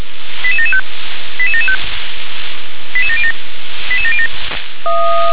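Five-tone selective call (ZVEI), heard as quick runs of five tones, tones of about 80 ms. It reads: 88775 88775 88707 88707
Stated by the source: ZVEI five-tone call received on SDR.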